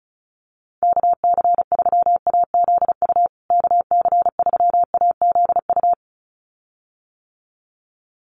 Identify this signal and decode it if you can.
Morse code, heard as a keyed tone, about 34 wpm, 700 Hz. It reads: KC3AZU KC3AZU
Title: USA Call Signs Repeated 2x 35wpm